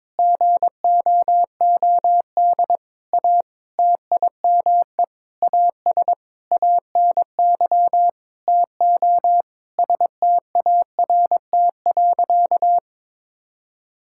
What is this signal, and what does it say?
Morse code, heard as a keyed tone, about 22 wpm, 700 Hz. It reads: GOOD A TIME AS ANY TO START.
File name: Sentences with 9-Word Phrases - 22wpm